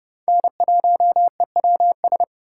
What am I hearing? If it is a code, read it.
Morse code, heard as a keyed tone, about 30 wpm, 700 Hz. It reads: N1EWS
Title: USA Call Signs - 5 Characters - 3x Character Spacing - 30wpm